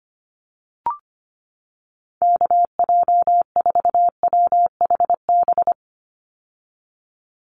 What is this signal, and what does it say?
Morse code, heard as a keyed tone, about 25 wpm, 690 Hz. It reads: KJ4WHB
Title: USA Call Signs 25wpm